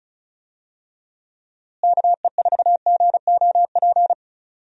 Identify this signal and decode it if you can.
Morse code, heard as a keyed tone, about 34 wpm, 700 Hz. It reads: KE4GOP
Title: USA Call Signs - 6 Characters - Repeated 3x - 35wpm